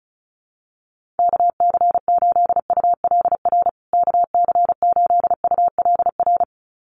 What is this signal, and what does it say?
Morse code, heard as a keyed tone, about 34 wpm, 700 Hz. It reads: KC8ULR KC8ULR
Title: USA Call Signs Repeated 2x 35wpm